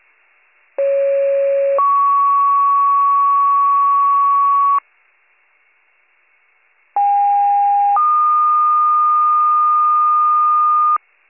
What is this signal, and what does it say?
Two-tone page, heard as a one-second tone, then a three-second tone, about 550 then 1120 Hz, about 810 then 1220 Hz. It is heard from 0.8 to 11.0 s.